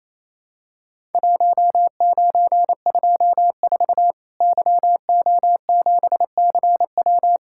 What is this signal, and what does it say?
Morse code, heard as a keyed tone, about 27 wpm, 700 Hz. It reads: 1924 YO7CW
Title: Worked All Europe DX Contest - QTC Exchange - 28wpm